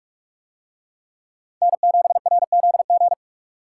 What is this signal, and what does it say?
Morse code, heard as a keyed tone, about 42 wpm, 700 Hz. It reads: N7RZG